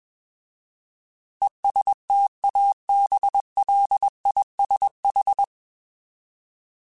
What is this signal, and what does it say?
Morse code, heard as a keyed tone, about 21 wpm, 800 Hz. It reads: ESTABLISH
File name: Alphabetical / Alphabetical mp3s / Es @20